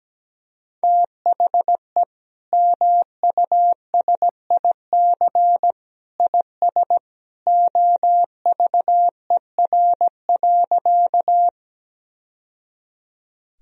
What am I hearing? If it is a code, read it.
Morse code, heard as a keyed tone, about 17 wpm, 700 Hz. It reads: THE MUSIC IS OVER.
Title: Sentences from Top 1000 Words - Review Part 1 - 17wpm